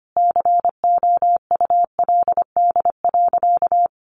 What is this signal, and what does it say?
Morse code, heard as a keyed tone, about 25 wpm, 700 Hz. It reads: COULD.
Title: Sentences from Top 500 Words 25wpm